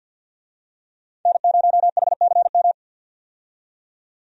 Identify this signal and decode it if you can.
Morse code, heard as a keyed tone, about 49 wpm, 700 Hz. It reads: N0HCM
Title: USA Call Signs - 5 Characters - 50wpm